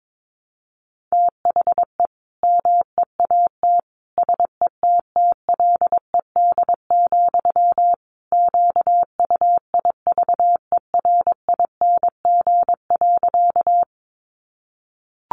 Morse code, heard as a keyed tone, about 22 wpm, 700 Hz. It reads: THE MEAT SETTLED, QUIVERING.
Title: A Sound Of Thunder 22wpm